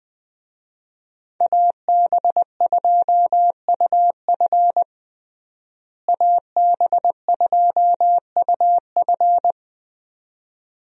Morse code, heard as a keyed tone, about 20 wpm, 700 Hz. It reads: AB2UF AB2UF
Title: USA Call Signs - 5 Characters - Repeated 2x - 20wpm